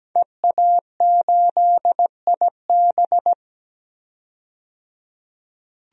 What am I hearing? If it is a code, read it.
Morse code, heard as a keyed tone, about 17 wpm, 700 Hz. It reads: EA8IB
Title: DXCC-100 Call Signs - 5 Characters - 17wpm